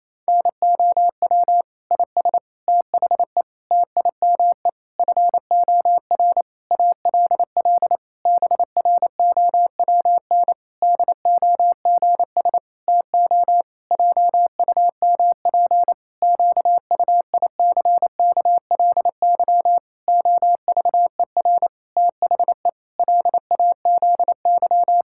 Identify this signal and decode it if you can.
Morse code, heard as a keyed tone, about 28 wpm, 700 Hz. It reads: NOW IS THE TIME FOR ALL BROWN DOGS TO JUMP QUICKLY OVER THE LAZY
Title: Pangram Challenge v2 - 28wpm